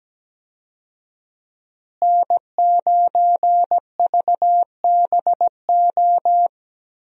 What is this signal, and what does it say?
Morse code, heard as a keyed tone, about 17 wpm, 700 Hz. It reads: N9VBO
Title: USA Call Signs - 5 Characters - Repeated 3x - 17wpm